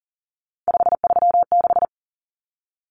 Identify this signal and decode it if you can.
Morse code, heard as a keyed tone, about 40 wpm, 700 Hz. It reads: 536
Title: Numbers - 3-Digit 40wpm